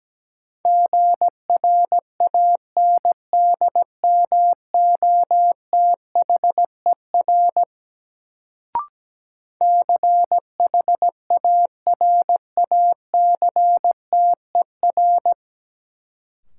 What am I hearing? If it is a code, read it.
Morse code, heard as a keyed tone, about 17 wpm, 700 Hz. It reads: GRANDMOTHER CHARACTER
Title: Sentences from Top 1200 Words - New Word Drill - 17wpm